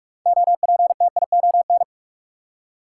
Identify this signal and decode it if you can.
Morse code, heard as a keyed tone, about 45 wpm, 700 Hz. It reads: OPTION
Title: Six Letter Words - 45wpm